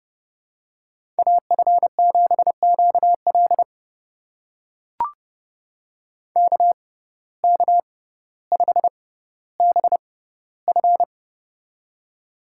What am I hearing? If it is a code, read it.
Morse code, heard as a keyed tone, about 30 wpm, spaced as for 15 wpm, 700 Hz. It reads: AF7QL KK5BF
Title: USA Call Signs - 5 Characters - 6x Character Spacing - 30wpm